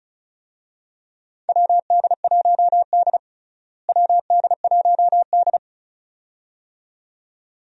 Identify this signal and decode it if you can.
Morse code, heard as a keyed tone, about 34 wpm, 700 Hz. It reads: WD1D WD1D
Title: USA Call Signs - 4 Characters - Repeated 2x - 35wpm